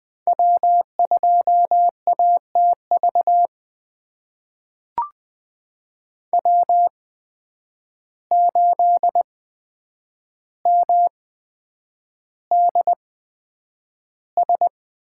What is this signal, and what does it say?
Morse code, heard as a keyed tone, about 20 wpm, spaced as for 9 wpm, 700 Hz. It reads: W2ATV W8MDS
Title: USA Call Signs - 8x Character Spacing - 20wpm